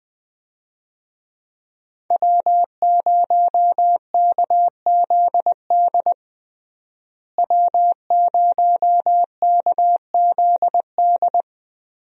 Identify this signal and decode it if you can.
Morse code, heard as a keyed tone, about 20 wpm, 700 Hz. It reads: W0KZD W0KZD
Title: USA Call Signs - 5 Characters - Repeated 2x - 20wpm